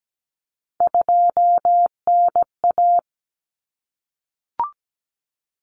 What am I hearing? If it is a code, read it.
Morse code, heard as a keyed tone, about 17 wpm, 700 Hz. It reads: 2NA
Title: Common Cut Numbers - 3-Digit 17wpm